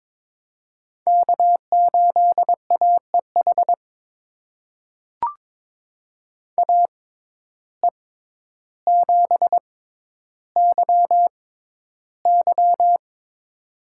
Morse code, heard as a keyed tone, about 22 wpm, spaced as for 11 wpm, 700 Hz. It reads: K8AEH AE7YY
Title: USA Call Signs - 5 Characters - 6x Character Spacing - 22wpm